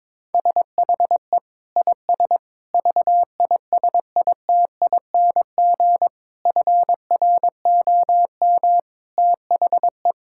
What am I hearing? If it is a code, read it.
Morse code, heard as a keyed tone, about 22 wpm, 700 Hz. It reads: SHE IS VISITING FROM THE
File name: practice / 22wpm / Sentences with 4-Word Phrases - 22wpm